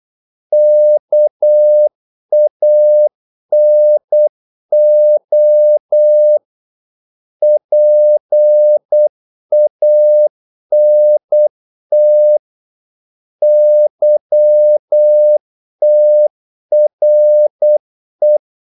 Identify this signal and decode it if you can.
Morse code, heard as a keyed tone, about 8 wpm, 600 Hz. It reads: KANO PANT YTRE
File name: Korte ord DK 8wpm